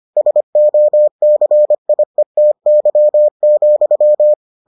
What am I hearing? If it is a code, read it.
Morse code, heard as a keyed tone, about 25 wpm, 600 Hz. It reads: SOCIETY,